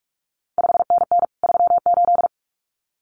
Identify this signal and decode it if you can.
Morse code, heard as a keyed tone, about 46 wpm, 700 Hz. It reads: 5NN 38
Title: CQ WW DX Contest Exchange Encore - 45wpm